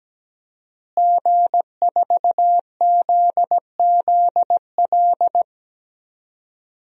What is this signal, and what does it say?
Morse code, heard as a keyed tone, about 17 wpm, 700 Hz. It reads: G4ZZL
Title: DXCC-100 Call Signs - 5 Characters - Repeated 3x - 17wpm